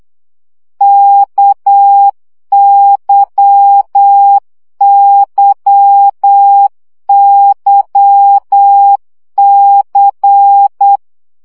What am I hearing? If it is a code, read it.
Morse code, heard as a keyed tone, about 8 wpm, 800 Hz. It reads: KYYYC
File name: CW gyakorlas k c y 7 baud